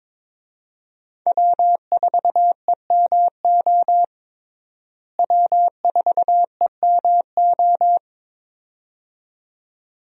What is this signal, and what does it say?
Morse code, heard as a keyed tone, about 22 wpm, 700 Hz. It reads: W4EMO W4EMO
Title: USA Call Signs - 5 Characters - Repeated 2x - 22wpm